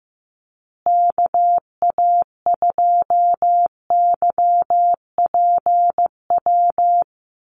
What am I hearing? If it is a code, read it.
Morse code, heard as a keyed tone, about 15 wpm, 700 Hz. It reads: KA2YPW
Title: USA Call Signs Repeated 3x 15wpm